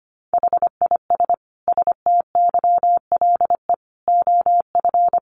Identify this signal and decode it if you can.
Morse code, heard as a keyed tone, about 25 wpm, 700 Hz. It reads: HIS STYLE OF
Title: Sentences from Top 800 Words 25wpm